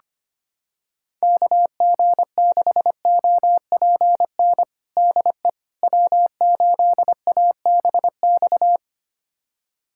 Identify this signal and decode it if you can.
Morse code, heard as a keyed tone, about 25 wpm, 700 Hz. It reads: KG6OPN DE W8ABX
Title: USA Callsign DE Callsign 25wpm